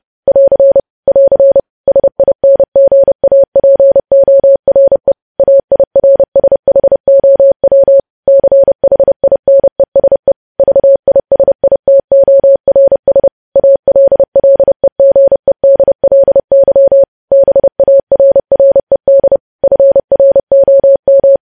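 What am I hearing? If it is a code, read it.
Morse code, heard as a keyed tone, about 30 wpm, 550 Hz. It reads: <AR> <AR> SINGAPORE AIRSHOW CHINESE VISITORS ALLEGEDLY BARRED FROM
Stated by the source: News headlines sent in Morse at 30 words per minute.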